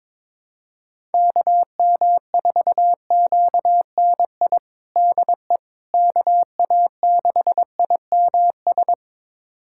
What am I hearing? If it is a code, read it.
Morse code, heard as a keyed tone, about 22 wpm, 700 Hz. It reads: KM4QNI DE KA6IMS
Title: USA Callsign DE Callsign - 6 Characters - 22wpm